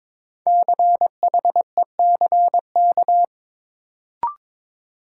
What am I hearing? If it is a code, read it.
Morse code, heard as a keyed tone, about 22 wpm, 700 Hz. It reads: CHECK
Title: Sentences from Top 1000 Words - Review Part 3 - New Word Drill - 22wpm